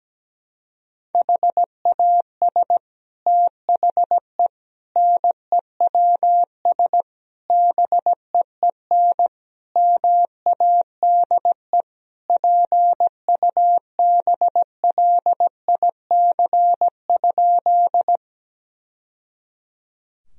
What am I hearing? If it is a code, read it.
Morse code, heard as a keyed tone, about 17 wpm, 700 Hz. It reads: HAS THE NEWS BEEN MADE PUBLIC?
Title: Sentences from Top 1000 Words - Review Part 3 - 17wpm